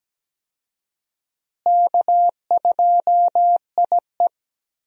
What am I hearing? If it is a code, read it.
Morse code, heard as a keyed tone, about 17 wpm, 700 Hz. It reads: K2IE
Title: USA Call Signs - 4 Characters - Repeated 3x - 17wpm